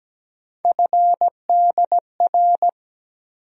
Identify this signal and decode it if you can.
Morse code, heard as a keyed tone, about 17 wpm, 700 Hz. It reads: FDR
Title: USA Call Sign Postfixes - 3 Letters 17wpm